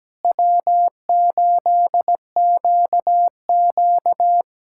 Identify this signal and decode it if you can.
Morse code, heard as a keyed tone, about 17 wpm, 700 Hz. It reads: W8QQ
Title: USA Callsign DE Callsign - 4 Characters - 17wpm